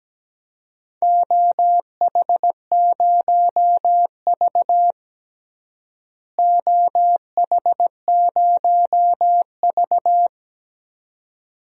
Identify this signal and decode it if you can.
Morse code, heard as a keyed tone, about 17 wpm, 700 Hz. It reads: OH0V OH0V
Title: DXCC-100 Call Signs - 4 Characters - Repeated 2x - 17wpm